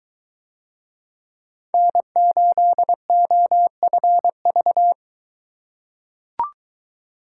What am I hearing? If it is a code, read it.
Morse code, heard as a keyed tone, about 22 wpm, 700 Hz. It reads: N8OFV